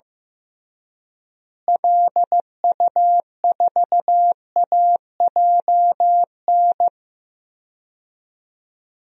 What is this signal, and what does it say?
Morse code, heard as a keyed tone, about 15 wpm, 700 Hz. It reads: LU4AJN
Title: DXCC-100 Call Signs - 6 Characters - Repeated 2x - 15wpm